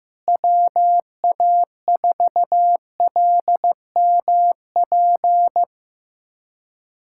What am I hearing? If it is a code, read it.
Morse code, heard as a keyed tone, about 15 wpm, 700 Hz. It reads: WA4LMP
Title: USA Callsign DE Callsign - 6 Characters - 15wpm